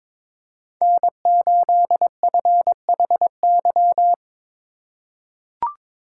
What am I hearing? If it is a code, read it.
Morse code, heard as a keyed tone, about 22 wpm, 700 Hz. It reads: N8FHY
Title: USA Call Signs - 5 Characters - 2x Character Spacing - 22wpm